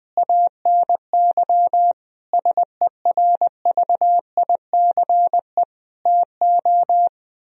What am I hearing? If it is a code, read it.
Morse code, heard as a keyed tone, about 19 wpm, 700 Hz. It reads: ANY SERVICE TO
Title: Sentences from Top 1000 Words - Review Part 4 - 20wpm